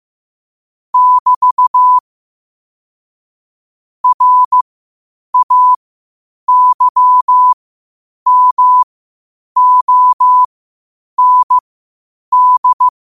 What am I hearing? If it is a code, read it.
Morse code, heard as a keyed tone, about 15 wpm, spaced as for 8 wpm, 1000 Hz. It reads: <BT> RAYMOND
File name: qotdfiles / 8wpm